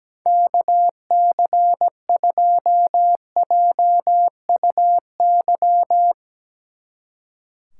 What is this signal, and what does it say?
Morse code, heard as a keyed tone, about 17 wpm, 700 Hz. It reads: KC2JUY
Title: Speed Racing - USA 2x3 Call Signs 17wpm